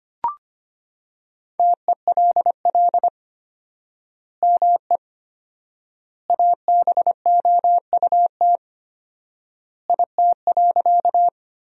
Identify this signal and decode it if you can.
Morse code, heard as a keyed tone, about 25 wpm, 700 Hz. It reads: TELL ME ABOUT IT.
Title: 4x Word-Spacing - Sentences from Top 200 Words - 25wpm